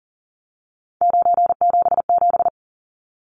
Morse code, heard as a keyed tone, about 40 wpm, 700 Hz. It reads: 977
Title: Numbers - 3-Digit 40wpm